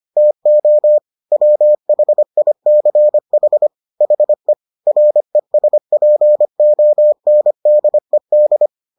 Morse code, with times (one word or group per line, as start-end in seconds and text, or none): 0.2-1.0 TO
1.3-3.7 WHICH
4.0-4.5 HE
4.9-8.7 RESPONDED